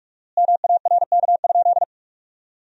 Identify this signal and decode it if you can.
Morse code, heard as a keyed tone, about 42 wpm, 700 Hz. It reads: MARK?
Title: 4x Word-Spacing - Sentences from Top 300 Words - 45wpm